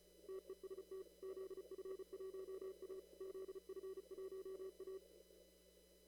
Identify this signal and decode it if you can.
Morse code, heard as a keyed tone, about 33 wpm, 380 Hz. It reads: TEST ZF1A ZF1A
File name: CW / kt5j wpx cw-op-k5pi